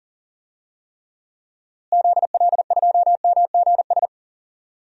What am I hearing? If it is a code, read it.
Morse code, heard as a keyed tone, about 40 wpm, 700 Hz. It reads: ZL2MGS
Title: DXCC-100 Call Signs - 6 Characters - Repeated 2x - 40wpm